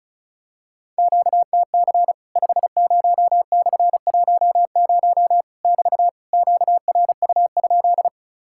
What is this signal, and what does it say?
Morse code, heard as a keyed tone, about 36 wpm, 700 Hz. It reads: QTC 50/10 <BT> QRU?
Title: Worked All Europe DX Contest - QTC Exchange - 35wpm